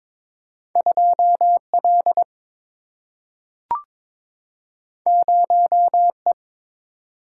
Morse code, heard as a keyed tone, about 22 wpm, 700 Hz. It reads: 2L 0E
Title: Single-Letter Single-Number 22wpm